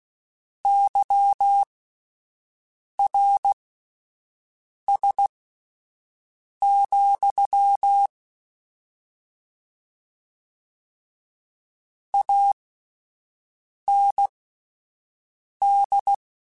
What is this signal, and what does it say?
Morse code, heard as a keyed tone, about 16 wpm, spaced as for 5 wpm, 800 Hz. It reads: YRS, AND
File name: QSOs / QSOs-mp3s / WD0BGP 1@13